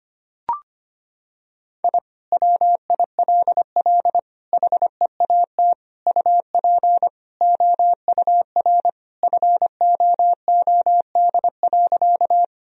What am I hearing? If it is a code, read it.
Morse code, heard as a keyed tone, about 25 wpm, 700 Hz. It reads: I WILL HEAT UP OUR FOOD.
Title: Sentences from Top 1000 Words - Review Part 3 - 25wpm